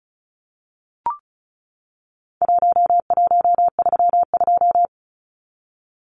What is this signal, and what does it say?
Morse code, heard as a keyed tone, about 34 wpm, 700 Hz. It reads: 1132
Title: Numbers - Contest Style - 4-Digit 35wpm